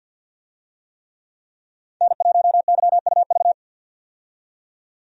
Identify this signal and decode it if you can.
Morse code, heard as a keyed tone, about 53 wpm, 700 Hz. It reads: N1YUV
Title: USA Call Signs - 5 Characters - 50wpm